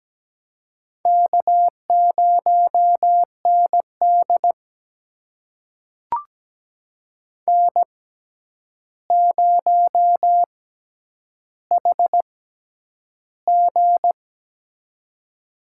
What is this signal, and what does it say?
Morse code, heard as a keyed tone, about 17 wpm, spaced as for 9 wpm, 700 Hz. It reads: K0ND N0HG
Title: USA Call Signs - 4 Characters - 6x Character Spacing - 17wpm